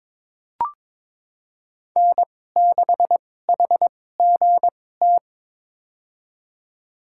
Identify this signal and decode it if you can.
Morse code, heard as a keyed tone, about 21 wpm, spaced as for 16 wpm, 700 Hz. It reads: N6HGT
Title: USA Call Signs - 5 Characters - 2x Character Spacing - 22wpm